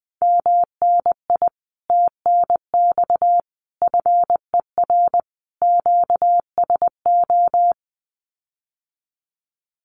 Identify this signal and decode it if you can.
Morse code, heard as a keyed tone, about 20 wpm, 700 Hz. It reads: MNI TNX FER QSO
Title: Common QSO Phrases Encore 20wpm